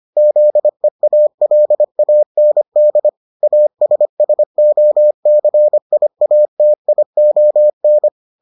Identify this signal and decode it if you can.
Morse code, heard as a keyed tone, about 25 wpm, 600 Hz. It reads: ZEALAND ASSOCIATION